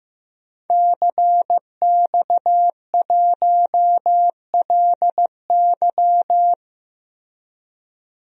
Morse code, heard as a keyed tone, about 15 wpm, 700 Hz. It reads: CX1LY
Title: DXCC-100 Call Signs - 5 Characters - Repeated 2x - 15wpm